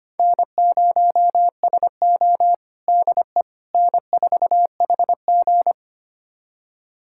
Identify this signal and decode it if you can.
Morse code, heard as a keyed tone, about 25 wpm, 700 Hz. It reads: N0SO DE N4HG
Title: USA Callsign DE Callsign - 4 Characters - 25wpm